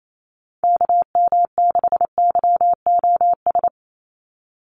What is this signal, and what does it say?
Morse code, heard as a keyed tone, about 28 wpm, 700 Hz. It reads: KM6YOS
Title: USA Call Signs 28wpm